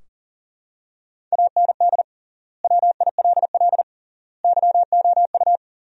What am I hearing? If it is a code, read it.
Morse code, heard as a keyed tone, about 40 wpm, 700 Hz. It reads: AND WILL YOU
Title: 3x Word-Spacing - Sentences from Top 400 Words - 40wpm